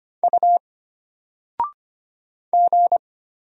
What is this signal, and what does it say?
Morse code, heard as a keyed tone, about 25 wpm, 700 Hz. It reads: U G
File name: Single-Letter-Number-25wpm